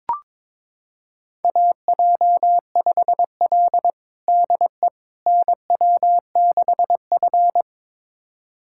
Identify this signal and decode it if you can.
Morse code, heard as a keyed tone, about 22 wpm, 700 Hz. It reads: AJ5L DE NW6F
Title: USA Callsign DE Callsign - 4 Characters - 22wpm